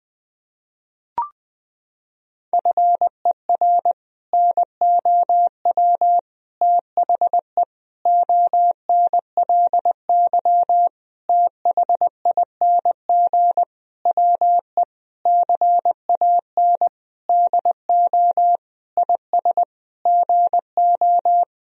Morse code, heard as a keyed tone, about 21 wpm, 700 Hz. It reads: FER NOW THE ONLY THING WE CAN DO IS GO
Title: Sentences from Top 200 Words and Abbreviations 20wpm